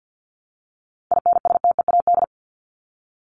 Morse code, heard as a keyed tone, about 50 wpm, 700 Hz. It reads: INSTEAD
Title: Top 501-750 Words 50wpm